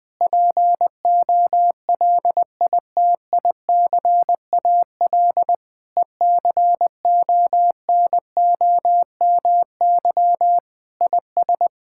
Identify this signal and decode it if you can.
Morse code, heard as a keyed tone, about 20 wpm, 700 Hz. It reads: POLITICAL ECONOMY IS
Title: Sentences from Top 1600 Words 20wpm